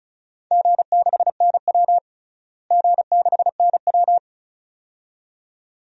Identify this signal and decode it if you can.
Morse code, heard as a keyed tone, about 36 wpm, 700 Hz. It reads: G6NW G6NW
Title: DXCC-100 Call Signs - 4 Characters - Repeated 2x - 35wpm